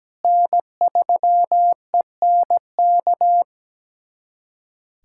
Morse code, heard as a keyed tone, about 17 wpm, 700 Hz. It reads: N3ENK
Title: USA Callsign DE Callsign - 5 Characters - 17wpm